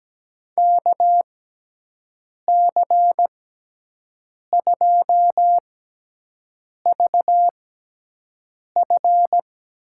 Morse code, heard as a keyed tone, about 17 wpm, spaced as for 6 wpm, 700 Hz. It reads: KC2VF
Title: USA Call Signs - 6 Characters - 6x Character Spacing - 17wpm